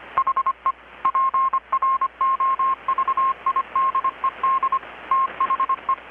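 Morse code, heard as a keyed tone, about 25 wpm, 1070 Hz. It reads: HE PROVIDED THE
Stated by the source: Morse from KPH's Night of Nights XX maritime radiotelegraphy event, received in Chiapas, Mexico about 3,800 km away on a Kenwood R-600 with a grounded 10 m long-wire antenna.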